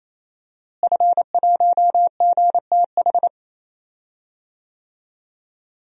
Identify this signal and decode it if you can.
Morse code, heard as a keyed tone, about 28 wpm, 700 Hz. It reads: F1GTH
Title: DXCC-100 Call Signs - 5 Characters - Repeated 3x - 28wpm